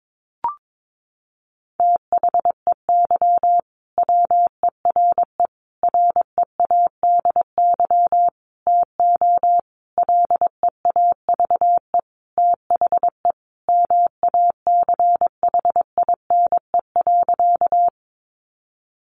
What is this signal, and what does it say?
Morse code, heard as a keyed tone, about 22 wpm, 700 Hz. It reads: THEY WERE READY TO LEAVE THE MACHINE.
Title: A Sound Of Thunder 22wpm